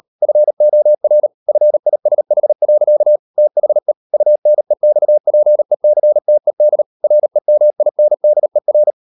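Morse code, heard as a keyed tone, about 38 wpm, 600 Hz. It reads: FOR FISH. THE UNEXPECTED REMINDER
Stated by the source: Morse code practice at 38 WPM, sending Ken Follett's novel Never in English, from CW med Gnister.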